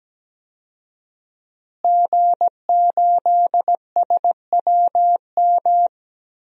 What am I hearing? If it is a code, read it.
Morse code, heard as a keyed tone, about 17 wpm, 700 Hz. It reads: G8SWM
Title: DXCC-100 Call Signs - 5 Characters - Repeated 3x - 17wpm